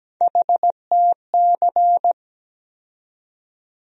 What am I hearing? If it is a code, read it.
Morse code, heard as a keyed tone, about 17 wpm, 700 Hz. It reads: HTC
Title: USA Call Sign Postfixes - 3 Letters 17wpm